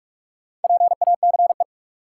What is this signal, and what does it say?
Morse code, heard as a keyed tone, about 46 wpm, 700 Hz. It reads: PACE
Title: Four Letter Words - 45wpm